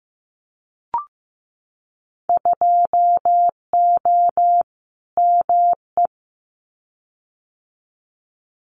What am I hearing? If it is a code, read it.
Morse code, heard as a keyed tone, about 15 wpm, 700 Hz. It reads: 2O ME
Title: Winter Field Day Exchanges 15wpm